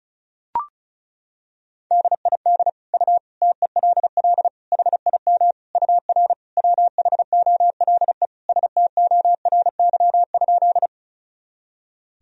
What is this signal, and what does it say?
Morse code, heard as a keyed tone, about 34 wpm, 700 Hz. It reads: DID U TELL HIM UR WHOLE STORY?
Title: Sentences from Top 400 Words and Abbreviations 35wpm